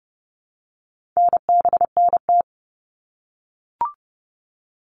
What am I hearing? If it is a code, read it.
Morse code, heard as a keyed tone, about 30 wpm, 700 Hz. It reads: NBNT
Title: All Cut Numbers - 4-Digit 30wpm